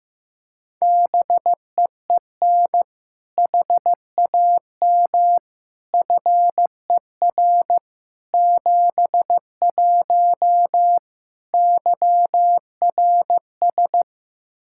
Morse code, heard as a keyed tone, about 16 wpm, 700 Hz. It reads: BEEN HAM FER 71 YRS
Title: Practice QSO - Exchange 3 - More Info - 15wpm